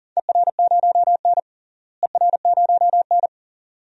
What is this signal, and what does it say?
Morse code, heard as a keyed tone, about 41 wpm, 700 Hz. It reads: ER0N ER0N
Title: DXCC-100 Call Signs - 4 Characters - Repeated 2x - 40wpm